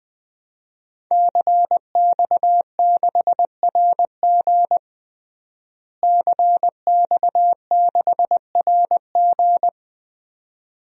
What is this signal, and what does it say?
Morse code, heard as a keyed tone, about 20 wpm, 700 Hz. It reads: CX6RG CX6RG
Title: DXCC-100 Call Signs - 5 Characters - Repeated 2x - 20wpm